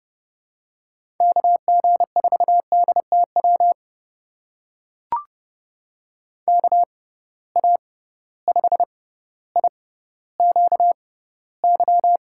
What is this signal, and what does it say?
Morse code, heard as a keyed tone, about 30 wpm, spaced as for 15 wpm, 700 Hz. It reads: KG4DTW KA5IQY
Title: USA Call Signs - 6 Characters - 6x Character Spacing - 30wpm